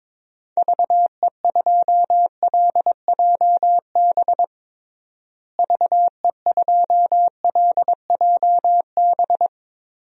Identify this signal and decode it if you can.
Morse code, heard as a keyed tone, about 22 wpm, 700 Hz. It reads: VE2LJB VE2LJB
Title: DXCC-100 Call Signs - 6 Characters - Repeated 2x - 22wpm